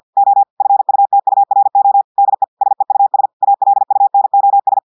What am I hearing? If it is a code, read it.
Morse code, heard as a keyed tone, about 50 wpm, 800 Hz. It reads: O FUTURO DE SEUS ALUNOS
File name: audios / portugais / portugais texte en clair 1 vitesse 50 mots minutes